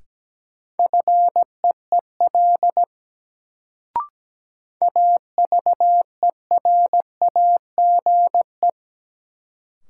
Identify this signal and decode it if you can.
Morse code, heard as a keyed tone, about 17 wpm, 700 Hz. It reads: FEEL AVERAGE
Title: Sets of 5 Words - New Word Drill - 17wpm